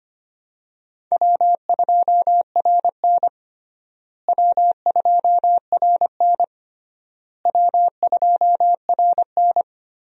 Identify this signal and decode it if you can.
Morse code, heard as a keyed tone, about 25 wpm, 700 Hz. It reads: W2RN W2RN W2RN
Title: USA Call Signs - 4 Characters - Repeated 3x - 25wpm